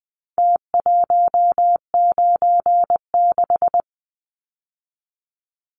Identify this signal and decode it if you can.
Morse code, heard as a keyed tone, about 20 wpm, 700 Hz. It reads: T196
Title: Numbers - Contest Style - 4-Digit 20wpm